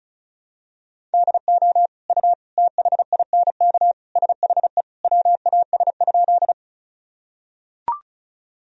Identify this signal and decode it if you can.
Morse code, heard as a keyed tone, about 36 wpm, 700 Hz. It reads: DO U THINK SHE WAS?
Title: Sentences from Top 200 Words and Abbreviations 35wpm